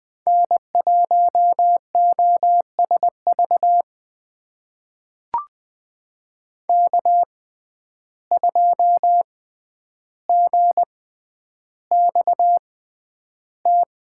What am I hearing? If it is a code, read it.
Morse code, heard as a keyed tone, about 20 wpm, spaced as for 10 wpm, 700 Hz. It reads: N1OSV K2GXT
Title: USA Call Signs - 5 Characters - 6x Character Spacing - 20wpm